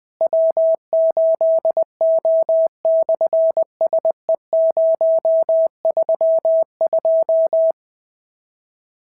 Morse code, heard as a keyed tone, about 19 wpm, 650 Hz. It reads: W8O/SE032
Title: SOTA QSO Practice 20wpm